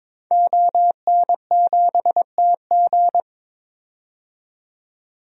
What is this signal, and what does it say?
Morse code, heard as a keyed tone, about 22 wpm, 700 Hz. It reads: ON7TG
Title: DXCC-100 Call Signs - 5 Characters - 22wpm